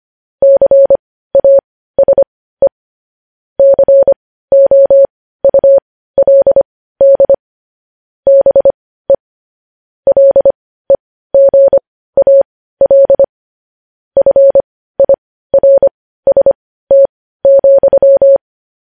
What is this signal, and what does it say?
Morse code, heard as a keyed tone, about 25 wpm, spaced as for 15 wpm, 550 Hz. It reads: CASE COULD BE LEGAL FIRST,
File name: podcast / News headlines-1x-25x15